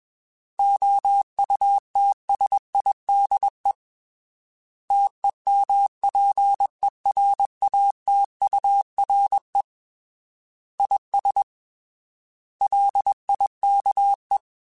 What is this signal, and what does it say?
Morse code, heard as a keyed tone, about 21 wpm, 800 Hz. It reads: OUTSIDE TEMPERATURE IS LIKE